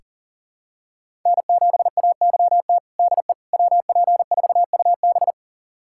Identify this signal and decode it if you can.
Morse code, heard as a keyed tone, about 40 wpm, 700 Hz. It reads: N7AYT DE WP4UB
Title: USA Callsign DE Callsign - 5 Characters - 40wpm